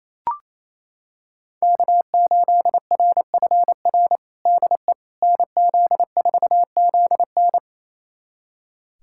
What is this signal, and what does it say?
Morse code, heard as a keyed tone, about 27 wpm, 700 Hz. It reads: K8RFR DE NZ4ZN
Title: USA Callsign DE Callsign - 5 Characters - 28wpm